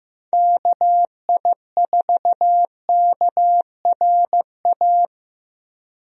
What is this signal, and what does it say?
Morse code, heard as a keyed tone, about 15 wpm, 700 Hz. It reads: KI4KRA
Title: USA Call Signs - 6 Characters - Repeated 3x - 15wpm